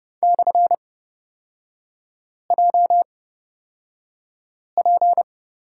Morse code, heard as a keyed tone, about 29 wpm, spaced as for 5 wpm, 690 Hz. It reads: /JP
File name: course / 30:10wpm / 236 Lesson 38e - Letter q - Three Characters 30wpm